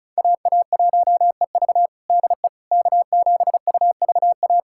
Morse code, heard as a keyed tone, about 36 wpm, 700 Hz. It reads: AA1EV DE K7UVA
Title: USA Callsign DE Callsign - 5 Characters - 35wpm